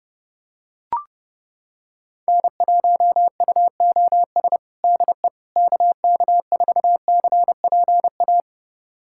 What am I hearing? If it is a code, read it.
Morse code, heard as a keyed tone, about 29 wpm, 700 Hz. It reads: N1UOS DE KK4CPA
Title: USA Callsign DE Callsign 30wpm